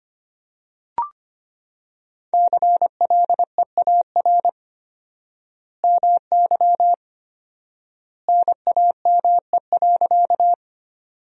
Morse code, heard as a keyed tone, about 25 wpm, 700 Hz. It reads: CLEAR MY NAME.
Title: 4x Word-Spacing - Sentences from Top 500 Words - 25wpm